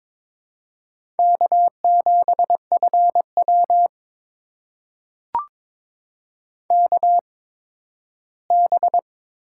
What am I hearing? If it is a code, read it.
Morse code, heard as a keyed tone, about 22 wpm, spaced as for 13 wpm, 700 Hz. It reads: K7FW KB
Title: USA Call Signs - 4 Characters - 8x Character Spacing - 22wpm